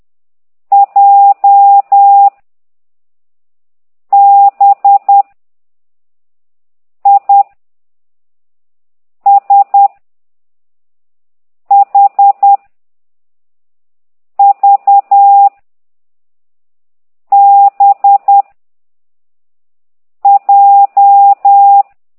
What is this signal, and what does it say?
Morse code, heard as a keyed tone, about 10 wpm, 800 Hz. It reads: J B I S H V B J
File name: exerc7